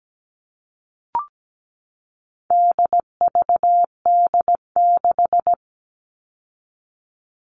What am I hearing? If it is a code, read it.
Morse code, heard as a keyed tone, about 17 wpm, 700 Hz. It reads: DVD6
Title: All Cut Numbers - 4-Digit 17wpm